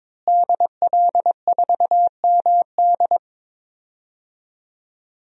Morse code, heard as a keyed tone, about 22 wpm, 700 Hz. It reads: DL4MD
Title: DXCC-100 Call Signs - 5 Characters - Repeated 3x - 22wpm